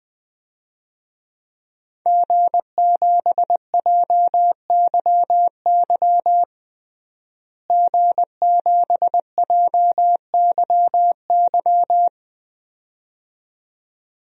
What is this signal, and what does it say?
Morse code, heard as a keyed tone, about 20 wpm, 700 Hz. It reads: G7JYY G7JYY
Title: DXCC-100 Call Signs - 5 Characters - Repeated 2x - 20wpm